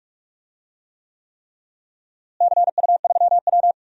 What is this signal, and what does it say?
Morse code, heard as a keyed tone, about 46 wpm, 700 Hz. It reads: CU3W
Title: DXCC-100 Call Signs - 4 Characters - Repeated 3x - 45wpm